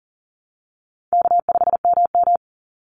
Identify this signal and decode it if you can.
Morse code, heard as a keyed tone, about 40 wpm, 700 Hz. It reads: K5MM
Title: Call Signs Repeated 2x - 40wpm